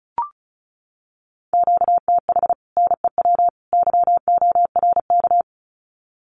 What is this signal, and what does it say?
Morse code, heard as a keyed tone, about 34 wpm, 700 Hz. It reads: QTH NEW YORK
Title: Common QSO Phrases 35wpm